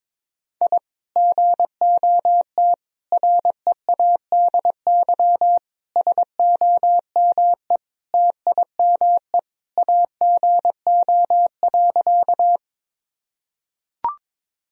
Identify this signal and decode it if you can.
Morse code, heard as a keyed tone, about 21 wpm, 700 Hz. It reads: I GOT READY SOME TIME AGO.
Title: Sentences from Top 1000 Words - Review Part 3 - 22wpm